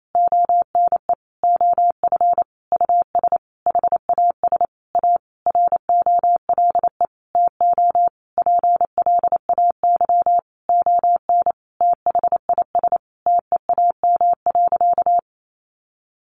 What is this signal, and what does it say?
Morse code, heard as a keyed tone, about 28 wpm, 700 Hz. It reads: ONE OF US HAS A ROLE TO PLAY ON THIS TEAM.
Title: Sentences from Top 1500 Words 28wpm